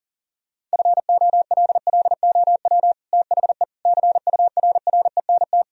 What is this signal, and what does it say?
Morse code, heard as a keyed tone, about 40 wpm, 700 Hz. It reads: FOLLOW THE CURRENT
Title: Sentences from Top 1000 Words - Review Part 4 - 40wpm